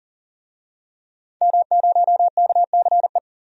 Morse code, heard as a keyed tone, about 40 wpm, 700 Hz. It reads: M0KCE
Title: DXCC-100 Call Signs - 5 Characters - 40wpm